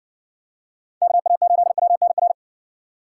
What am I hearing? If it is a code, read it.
Morse code, heard as a keyed tone, about 59 wpm, 700 Hz. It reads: KA8WNR